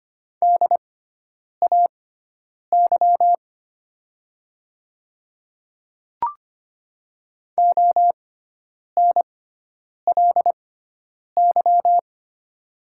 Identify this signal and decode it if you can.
Morse code, heard as a keyed tone, about 25 wpm, spaced as for 9 wpm, 700 Hz. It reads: DAY ONLY
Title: 6x Character Spacing - Top 300 Words - 25wpm